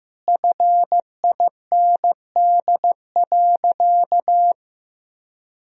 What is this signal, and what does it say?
Morse code, heard as a keyed tone, about 15 wpm, 700 Hz. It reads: FIND.
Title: Sentences from Top 1000 Words - Review Part 2 - 15wpm